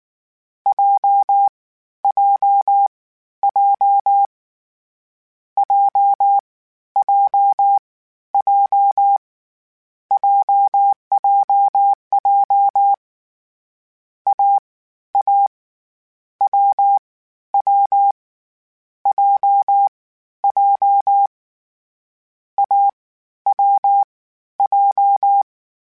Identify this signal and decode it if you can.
Morse code, heard as a keyed tone, about 19 wpm, spaced as for 12 wpm, 800 Hz. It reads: JJJ JJJ JJJ AA WW JJ AWJ
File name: Difficult Combos / Difficult Combos-mp3s / AWJ1 @18